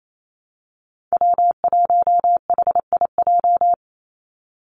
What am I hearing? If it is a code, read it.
Morse code, heard as a keyed tone, about 28 wpm, 700 Hz. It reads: W1HIJ
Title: Call Signs Repeated 2x - 28wpm